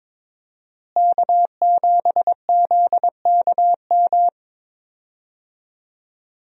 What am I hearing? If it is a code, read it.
Morse code, heard as a keyed tone, about 22 wpm, 700 Hz. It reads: K7ZKM
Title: USA Call Signs - 5 Characters - Repeated 2x - 22wpm